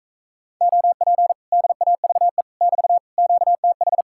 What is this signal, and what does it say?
Morse code, heard as a keyed tone, about 44 wpm, 700 Hz. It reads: OP DAVE <BT> QTH